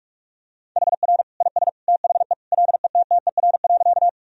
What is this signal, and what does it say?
Morse code, heard as a keyed tone, about 46 wpm, 700 Hz. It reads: HR IS THE LETTER.
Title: Sentences from Top 300 Words and Abbreviations 45wpm